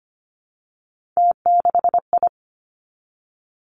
Morse code, heard as a keyed tone, about 25 wpm, 700 Hz. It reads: T6I
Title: Three Challenging Letter-Number Combinations 25wpm